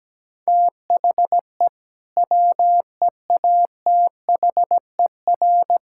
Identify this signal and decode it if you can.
Morse code, heard as a keyed tone, about 17 wpm, 700 Hz. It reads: THE WEATHER
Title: Sentences from Top 1000 Words - Review Part 5 - 17wpm